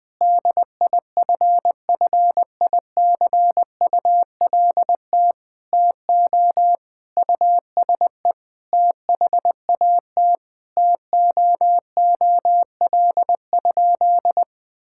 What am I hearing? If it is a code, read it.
Morse code, heard as a keyed tone, about 20 wpm, 700 Hz. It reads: DIFFICULT TO USE THAT TOOL?
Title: Sentences from Top 1000 Words - Review Part 6 - 20wpm